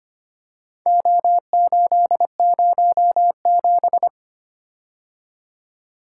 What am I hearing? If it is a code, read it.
Morse code, heard as a keyed tone, about 25 wpm, 700 Hz. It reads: O807
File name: practice / 25wpm / Four Letter-Number Combinations - 25wpm